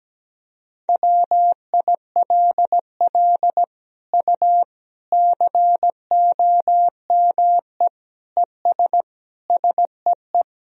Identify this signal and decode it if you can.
Morse code, heard as a keyed tone, about 17 wpm, 700 Hz. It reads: WILL U COME ES SEE
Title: Sentences from Top 100 Words and Abbreviations 17wpm